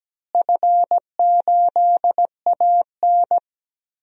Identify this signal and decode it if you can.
Morse code, heard as a keyed tone, about 17 wpm, 700 Hz. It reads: F8AN
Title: DXCC-100 Call Signs - 4 Characters - Repeated 2x - 17wpm